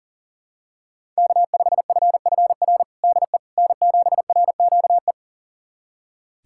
Morse code, heard as a keyed tone, about 37 wpm, 700 Hz. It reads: K5FFR DE N7RQE
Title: USA Callsign DE Callsign - 5 Characters - 40wpm